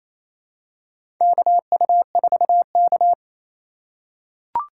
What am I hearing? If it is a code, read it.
Morse code, heard as a keyed tone, about 28 wpm, 700 Hz. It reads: KU4K
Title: USA Call Signs - 4 Characters - 3x Character Spacing - 28wpm